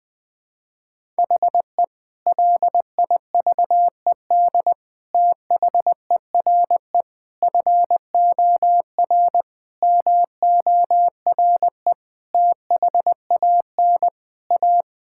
Morse code, heard as a keyed tone, about 21 wpm, 700 Hz. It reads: HE LIVED THERE FOR MORE THAN A
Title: Sentences with 5-Word Phrases - Spoken Beginning - 20wpm